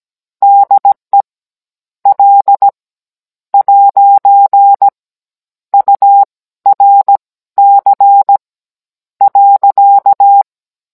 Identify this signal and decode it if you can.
Morse code, heard as a keyed tone, about 17 wpm, spaced as for 13 wpm, 800 Hz. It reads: DE L ' URC .
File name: Cours-NR18-16mm